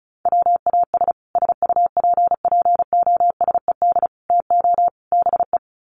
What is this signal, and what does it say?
Morse code, heard as a keyed tone, about 34 wpm, 700 Hz. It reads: WAS SUPPOSED TO BE
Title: Sets of 4 Words 35wpm